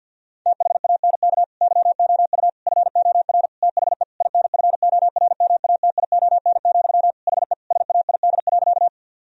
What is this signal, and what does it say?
Morse code, heard as a keyed tone, about 50 wpm, 700 Hz. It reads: THANK YOU FOR THE INFORMATION, HE SAID.
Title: Sentences from Top 1000 Words - Review Part 3 - 50wpm